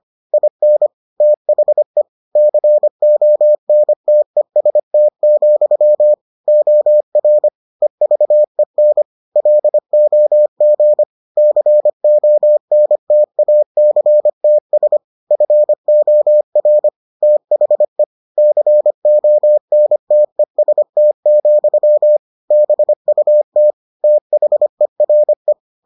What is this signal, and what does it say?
Morse code, heard as a keyed tone, about 25 wpm, 600 Hz. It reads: IN THE CONTEST, OR EVEN LOG CONTACTS FOR THE CONTEST, BUT THERE